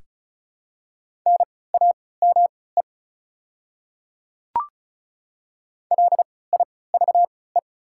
Morse code, heard as a keyed tone, about 36 wpm, spaced as for 20 wpm, 700 Hz. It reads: NAME LIVE
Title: 3x Character Spacing - Top 200 Words - 35wpm